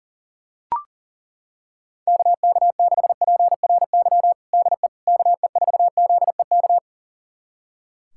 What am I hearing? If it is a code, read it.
Morse code, heard as a keyed tone, about 40 wpm, 700 Hz. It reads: KK6PRY DE KE4ZEK
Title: USA Callsign DE Callsign - 6 Characters - 40wpm